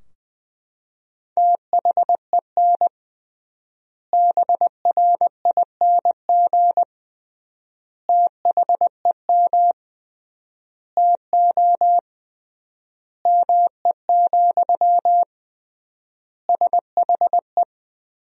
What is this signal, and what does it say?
Morse code, heard as a keyed tone, about 20 wpm, 700 Hz. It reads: THEN BRING THEM TO ME, SHE
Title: 3x Word-Spacing - Sentences from Top 500 Words - 20wpm